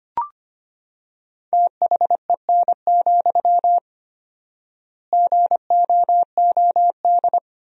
Morse code, heard as a keyed tone, about 25 wpm, 700 Hz. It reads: THEN, GOOD
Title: 4x Word-Spacing - Sentences from Top 300 Words - 25wpm